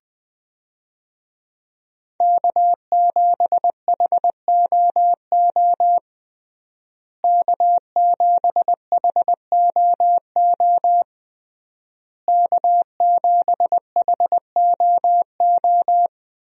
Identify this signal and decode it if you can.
Morse code, heard as a keyed tone, about 21 wpm, 700 Hz. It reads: K7HOO K7HOO K7HOO
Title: USA Call Signs - 5 Characters - Repeated 3x - 20wpm